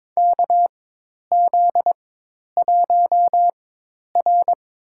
Morse code, heard as a keyed tone, about 23 wpm, spaced as for 10 wpm, 700 Hz. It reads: KZ1R
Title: USA Call Signs - 4 Characters - 4x Character Spacing - 22wpm